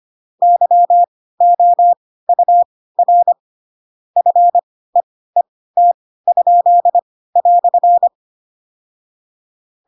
Morse code, heard as a keyed tone, about 25 wpm, spaced as for 16 wpm, 700 Hz. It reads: YOUR FEET?"
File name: Pinocchio / Pinocchio ch7 10 25 ik1hge